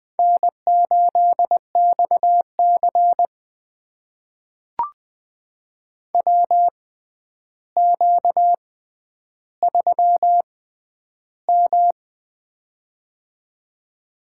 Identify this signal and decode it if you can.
Morse code, heard as a keyed tone, about 20 wpm, spaced as for 10 wpm, 700 Hz. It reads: N8XC WQ3M
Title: USA Call Signs - 4 Characters - 6x Character Spacing - 20wpm